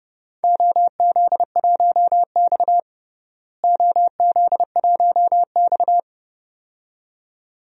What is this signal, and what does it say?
Morse code, heard as a keyed tone, about 30 wpm, 700 Hz. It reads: OZ1X OZ1X
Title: DXCC-100 Call Signs - 4 Characters - Repeated 2x - 30wpm